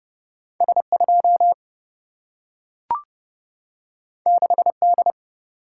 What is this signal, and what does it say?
Morse code, heard as a keyed tone, about 30 wpm, 700 Hz. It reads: S2 6D
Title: Single-Letter Single-Number 30wpm